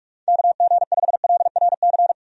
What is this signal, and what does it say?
Morse code, heard as a keyed tone, about 47 wpm, 700 Hz. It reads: KG5LRC
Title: USA Call Signs - 6 Characters - 45wpm